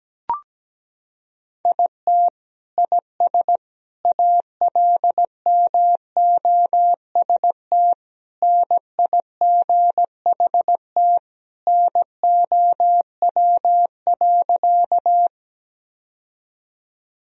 Morse code, heard as a keyed tone, about 17 wpm, 700 Hz. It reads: IT IS ALMOST NIGHT NOW.
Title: Sentences from Top 1000 Words - Review Part 4 - 17wpm